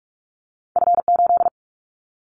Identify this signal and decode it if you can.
Morse code, heard as a keyed tone, about 46 wpm, 700 Hz. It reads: F8
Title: Callsign Prefixes for Top 25 DXCC - 45wpm